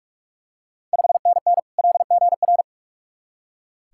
Morse code, heard as a keyed tone, about 46 wpm, 690 Hz. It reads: 5NN LGR
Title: Ontario QSO Party Exchange 45wpm